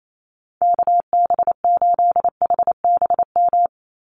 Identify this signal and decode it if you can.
Morse code, heard as a keyed tone, about 28 wpm, 700 Hz. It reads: KB8HBM
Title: USA Call Signs 28wpm